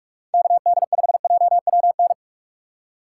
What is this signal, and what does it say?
Morse code, heard as a keyed tone, about 43 wpm, 690 Hz. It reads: KD5JWN